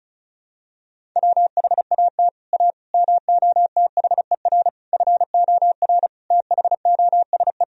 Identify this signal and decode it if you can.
Morse code, heard as a keyed tone, about 34 wpm, 700 Hz. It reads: WHAT A MOTHER FOR THOSE
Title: Sentences from Top 1000 Words - Review Part 1 - 35wpm